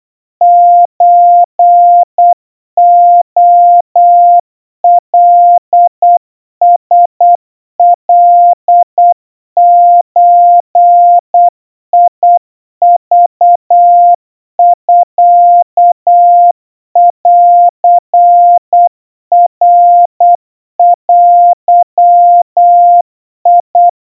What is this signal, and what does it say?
Morse code (Japanese Wabun code), heard as a keyed tone, about 8 wpm, 700 Hz. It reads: ソレカラカゾクミンナデ
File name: 30 cpm 700 Hz Crispy